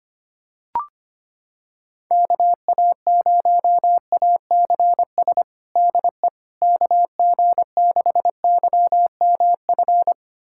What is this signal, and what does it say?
Morse code, heard as a keyed tone, about 25 wpm, 700 Hz. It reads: KA0ACS DE KG6YMF
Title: USA Callsign DE Callsign 25wpm